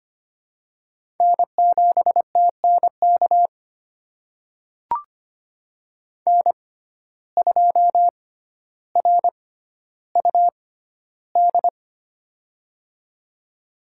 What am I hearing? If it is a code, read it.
Morse code, heard as a keyed tone, about 25 wpm, spaced as for 13 wpm, 700 Hz. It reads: N7TNK N2RUD
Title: USA Call Signs - 5 Characters - 6x Character Spacing - 25wpm